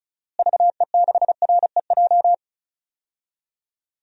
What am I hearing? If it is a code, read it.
Morse code, heard as a keyed tone, about 36 wpm, 700 Hz. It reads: VE6REJ
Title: DXCC-100 Call Signs - 6 Characters - Repeated 3x - 35wpm